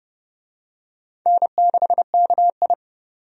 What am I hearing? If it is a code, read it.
Morse code, heard as a keyed tone, about 30 wpm, 700 Hz. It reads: N6KI
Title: USA Call Signs - 4 Characters - 30wpm